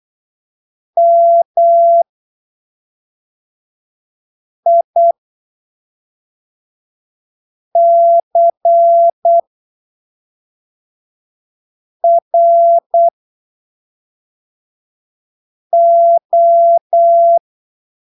Morse code, heard as a keyed tone, about 8 wpm, spaced as for 3 wpm, 680 Hz. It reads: MICRO